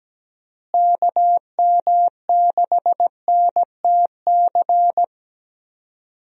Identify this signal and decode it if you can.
Morse code, heard as a keyed tone, about 17 wpm, 700 Hz. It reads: KM6NTC
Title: USA Call Signs - 6 Characters - Repeated 3x - 17wpm